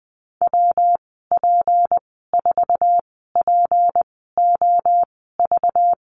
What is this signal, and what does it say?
Morse code, heard as a keyed tone, about 20 wpm, spaced as for 15 wpm, 700 Hz. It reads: WP4POV
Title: 193 Lesson 31g - Letter v - Call Signs 20wpm